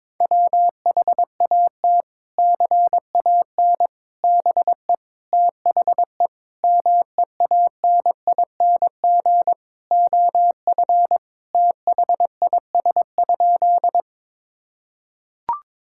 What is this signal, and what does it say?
Morse code, heard as a keyed tone, about 22 wpm, 700 Hz. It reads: WHAT CAN BE THE MEANING OF THIS?
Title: Sentences from Top 1000 Words - Review Part 6 - 22wpm